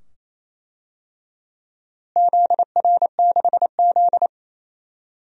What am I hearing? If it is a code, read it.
Morse code, heard as a keyed tone, about 28 wpm, 700 Hz. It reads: ZR6Z
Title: DXCC-100 Call Signs - 4 Characters - Repeated 2x - 28wpm